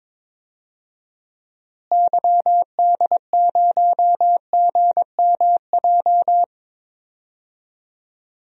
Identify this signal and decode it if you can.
Morse code, heard as a keyed tone, about 21 wpm, 700 Hz. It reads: YD0GMJ
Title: DXCC-100 Call Signs - 6 Characters - Repeated 2x - 22wpm